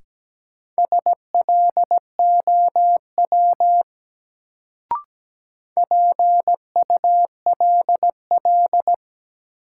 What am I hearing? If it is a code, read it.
Morse code, heard as a keyed tone, about 17 wpm, 700 Hz. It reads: SLOW PULL
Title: Sentences from Top 400 Words - New Word Drill - 17wpm